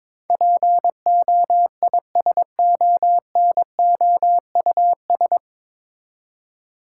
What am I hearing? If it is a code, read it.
Morse code, heard as a keyed tone, about 22 wpm, 700 Hz. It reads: POISONOUS
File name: 9 Letter Words - First 3 Letters Given - 22wpm